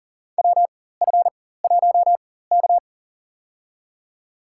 Morse code, heard as a keyed tone, about 40 wpm, spaced as for 19 wpm, 700 Hz. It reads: WF1K
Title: USA Call Signs - 4x Character Spacing - 40wpm